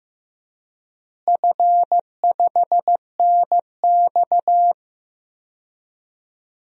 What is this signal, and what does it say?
Morse code, heard as a keyed tone, about 15 wpm, 700 Hz. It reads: F5NX
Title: DXCC-100 Call Signs - 4 Characters - Repeated 2x - 15wpm